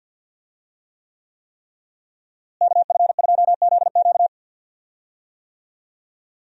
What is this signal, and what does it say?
Morse code, heard as a keyed tone, about 51 wpm, 700 Hz. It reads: KF2ZQ
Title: USA Call Signs - 5 Characters - Repeated 3x - 50wpm